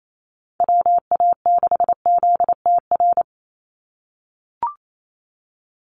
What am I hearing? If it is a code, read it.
Morse code, heard as a keyed tone, about 28 wpm, 700 Hz. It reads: WA6ZTR
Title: USA Call Signs 28wpm